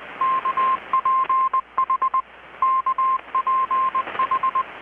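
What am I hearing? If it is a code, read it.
Morse code, heard as a keyed tone, about 20 wpm, 1050 Hz, with radio noise. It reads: KPH KPH
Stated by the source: Morse from KPH's Night of Nights XX maritime radiotelegraphy event, received in Chiapas, Mexico about 3,800 km away on a Kenwood R-600 with a grounded 10 m long-wire antenna.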